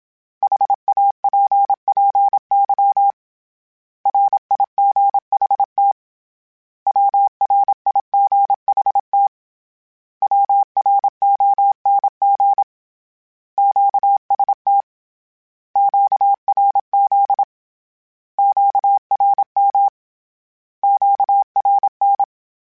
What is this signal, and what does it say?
Morse code, heard as a keyed tone, about 27 wpm, 800 Hz. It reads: HAPPY RIGHT WRIGHT WRONG QST QRZ QRM QRN
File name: Short Runs / Short Runs-mp3s / Words @25